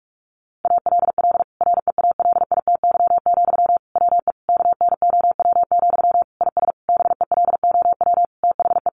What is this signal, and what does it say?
Morse code, heard as a keyed tone, about 44 wpm, 700 Hz. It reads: ALL REALITY, WE KNOW, IS BELOW THE
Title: Sentences from Top 1400 Words 45wpm